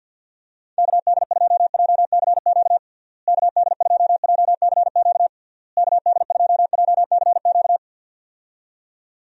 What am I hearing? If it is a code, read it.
Morse code, heard as a keyed tone, about 51 wpm, 700 Hz. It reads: KD2JCQ KD2JCQ KD2JCQ
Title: USA Call Signs - 6 Characters - Repeated 3x - 50wpm